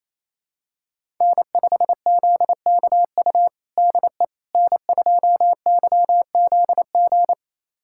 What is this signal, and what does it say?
Morse code, heard as a keyed tone, about 28 wpm, 700 Hz. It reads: N5ZKU DE N2YZG
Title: USA Callsign DE Callsign - 5 Characters - 28wpm